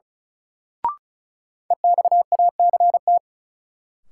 Morse code, heard as a keyed tone, about 34 wpm, 700 Hz. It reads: EXACT
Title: Sentences from Top 1300 Words - New Word Drill Alt - 35wpm